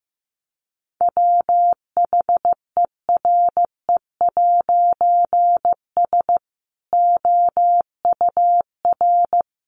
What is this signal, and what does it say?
Morse code, heard as a keyed tone, about 15 wpm, 700 Hz. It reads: WHERE'S OUR
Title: A Sound Of Thunder 15wpm